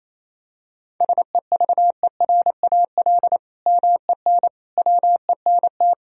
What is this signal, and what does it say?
Morse code, heard as a keyed tone, about 28 wpm, 700 Hz. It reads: SEVERAL MEN WENT
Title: Sentences from Top 1000 Words - Review Part 2 - 28wpm